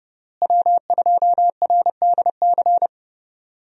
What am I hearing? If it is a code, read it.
Morse code, heard as a keyed tone, about 30 wpm, 700 Hz. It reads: W2RDC
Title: USA Call Signs - 5 Characters - Repeated 3x - 30wpm